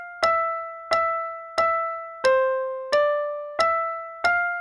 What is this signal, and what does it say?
There is no code.